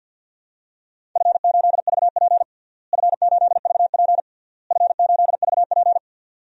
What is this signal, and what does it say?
Morse code, heard as a keyed tone, about 48 wpm, 700 Hz. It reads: F8VP F8VP F8VP
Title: DXCC-100 Call Signs - 4 Characters - Repeated 3x - 50wpm